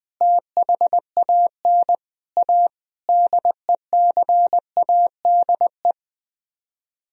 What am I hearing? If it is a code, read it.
Morse code, heard as a keyed tone, about 20 wpm, 700 Hz. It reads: THAN A DECADE